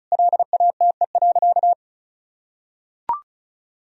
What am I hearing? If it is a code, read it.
Morse code, heard as a keyed tone, about 34 wpm, 700 Hz. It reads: LATE.
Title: Sentences from Top 1000 Words - Review Part 4 - 35wpm